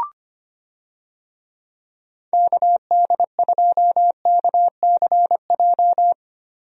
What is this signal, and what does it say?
Morse code, heard as a keyed tone, about 25 wpm, 700 Hz. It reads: KD2KCJ
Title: USA Call Signs - 6 Characters - 25wpm